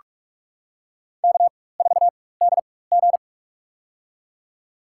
Morse code, heard as a keyed tone, about 47 wpm, spaced as for 21 wpm, 700 Hz. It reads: K4DG
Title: USA Call Signs - 4x Character Spacing - 45wpm